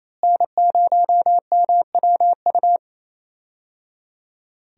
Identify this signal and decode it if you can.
Morse code, heard as a keyed tone, about 28 wpm, 700 Hz. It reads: N0MWU
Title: USA Call Signs - 5 Characters - Repeated 3x - 28wpm